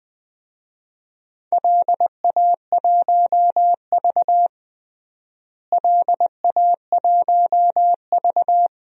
Morse code, heard as a keyed tone, about 20 wpm, 700 Hz. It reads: LA1V LA1V
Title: DXCC-100 Call Signs - 4 Characters - Repeated 3x - 20wpm